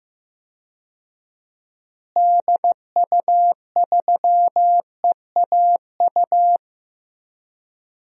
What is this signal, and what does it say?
Morse code, heard as a keyed tone, about 16 wpm, 700 Hz. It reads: DU3EAU
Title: DXCC-100 Call Signs - 6 Characters - Repeated 2x - 15wpm